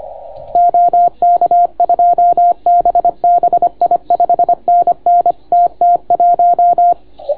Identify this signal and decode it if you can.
Morse code, heard as a keyed tone, about 25 wpm, 690 Hz. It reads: OK2BBI 5NN TT1